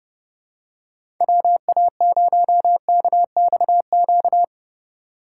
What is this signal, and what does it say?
Morse code, heard as a keyed tone, about 31 wpm, 700 Hz. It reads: WA0KXQ
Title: USA Call Signs - 6 Characters - 3x Character Spacing - 30wpm